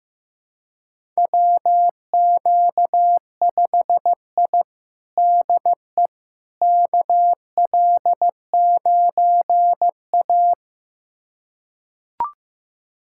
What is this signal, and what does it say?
Morse code, heard as a keyed tone, about 16 wpm, 700 Hz. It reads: WQ5I DE KL9A
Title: USA Callsign DE Callsign - 4 Characters - 15wpm